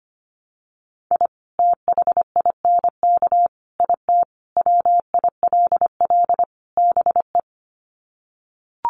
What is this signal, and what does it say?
Morse code, heard as a keyed tone, about 25 wpm, 700 Hz. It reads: I THINK IT WILL BE
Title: Sets of 5 Words Encore2 25wpm